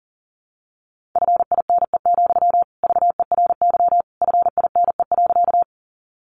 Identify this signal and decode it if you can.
Morse code, heard as a keyed tone, about 40 wpm, 700 Hz. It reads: FINE, VERY FINE.
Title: Sentences from Top 500 Words 40wpm